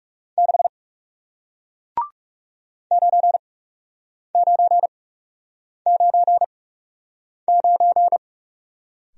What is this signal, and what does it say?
Morse code, heard as a keyed tone, about 38 wpm, spaced as for 8 wpm, 700 Hz. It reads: 6 9999